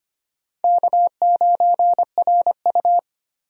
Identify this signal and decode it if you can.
Morse code, heard as a keyed tone, about 25 wpm, 700 Hz. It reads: K9RU
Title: USA Call Signs - 4 Characters - Repeated 2x - 25wpm